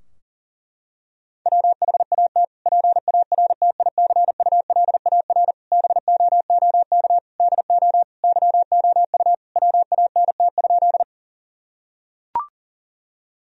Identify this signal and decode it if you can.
Morse code, heard as a keyed tone, about 40 wpm, 690 Hz. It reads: WHAT PARTICULAR BOOK DO YOU WANT?
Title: Sentences from Top 1000 Words - Review Part 4 - 40wpm